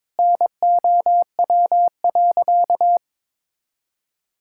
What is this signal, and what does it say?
Morse code, heard as a keyed tone, about 22 wpm, 700 Hz. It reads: NOW.
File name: practice / 22wpm / Sentences with 2-Word Phrases - 22wpm